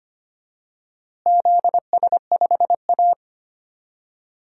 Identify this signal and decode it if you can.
Morse code, heard as a keyed tone, about 25 wpm, 700 Hz. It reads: ZS5A
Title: DXCC-100 Call Signs - 4 Characters - Repeated 2x - 25wpm